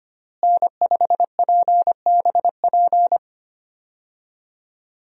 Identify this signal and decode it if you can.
Morse code, heard as a keyed tone, about 25 wpm, 700 Hz. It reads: N5PBP
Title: USA Call Signs - 5 Characters - Repeated 3x - 25wpm